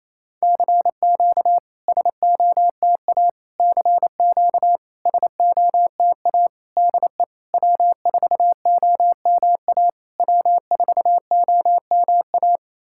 Morse code, heard as a keyed tone, about 28 wpm, 690 Hz. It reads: CQ SOTA CQ SOTA DE W4OMA W4OMA
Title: SOTA QSO Practice 28wpm